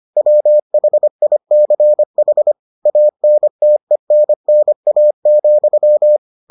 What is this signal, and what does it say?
Morse code, heard as a keyed tone, about 25 wpm, 600 Hz. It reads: WHICH ANTENNA,